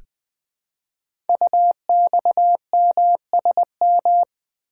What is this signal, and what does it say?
Morse code, heard as a keyed tone, about 20 wpm, 700 Hz. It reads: UXMSM
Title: Code Groups - 5 Letters - 20wpm